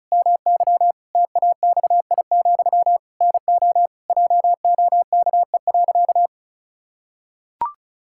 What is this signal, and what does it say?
Morse code, heard as a keyed tone, about 34 wpm, 700 Hz. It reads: MY TAXI, NO JOKE.
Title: Pangram Challenge v2 - 35wpm